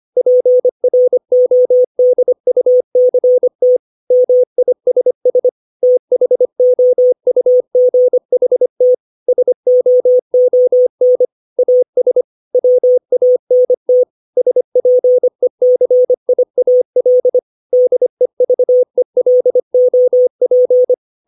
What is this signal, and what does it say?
Morse code, heard as a keyed tone, about 25 wpm, 500 Hz. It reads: PRODUCT MISS THOUGHT SOON AS WANT SPECIAL DEVELOP
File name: 632 words / 632 Words 25WPM